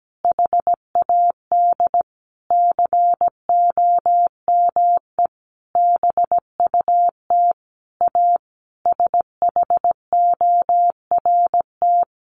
Morse code, heard as a keyed tone, about 17 wpm, 700 Hz. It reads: HAD COME BUT A SHORT
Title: Sentences from Top 1400 Words 17wpm